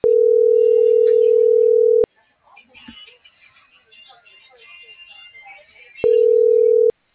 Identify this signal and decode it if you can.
Ringback tone (North America), 440 + 480 Hz.
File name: CENTRAL MAYAB / out vianey.prado - .62783